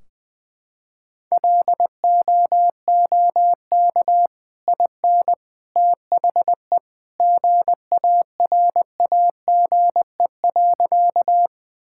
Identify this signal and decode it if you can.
Morse code, heard as a keyed tone, about 20 wpm, 700 Hz. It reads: LOOK IN THE GARAGE.
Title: Sentences from Top 1600 Words 20wpm